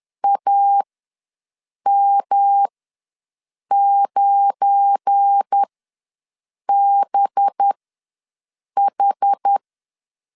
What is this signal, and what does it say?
Morse code, heard as a keyed tone, about 10 wpm, spaced as for 6 wpm, 800 Hz. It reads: AM9BH
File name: LEZIONI / 7 GF9Z / 7 GF9 ws 10wpm 4034